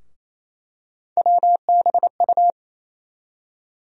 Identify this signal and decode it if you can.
Morse code, heard as a keyed tone, about 28 wpm, 700 Hz. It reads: WBU
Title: USA Call Sign Postfixes - 3 Letters 28wpm